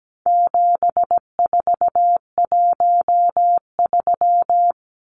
Morse code, heard as a keyed tone, about 17 wpm, 700 Hz. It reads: 7413
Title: Numbers - 4-Digit 17wpm